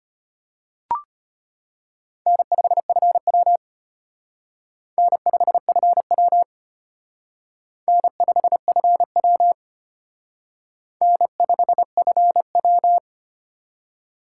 Morse code, heard as a keyed tone, about 32 wpm, 700 Hz. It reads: N5FW N5FW N5FW N5FW